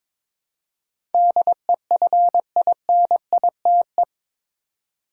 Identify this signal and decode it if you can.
Morse code, heard as a keyed tone, about 21 wpm, 700 Hz. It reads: DEFINITE
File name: practice / 22wpm / Eight Letter Words - 22wpm